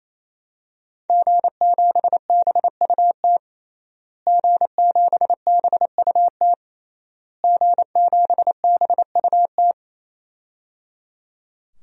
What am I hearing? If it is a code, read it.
Morse code, heard as a keyed tone, about 28 wpm, 700 Hz. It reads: G7BUT G7BUT G7BUT
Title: DXCC-100 Call Signs - 5 Characters - Repeated 3x - 28wpm